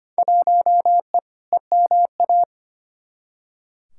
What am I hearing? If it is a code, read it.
Morse code, heard as a keyed tone, about 25 wpm, 700 Hz. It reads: 1E EMA
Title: Field Day Exchanges Encore - 25wpm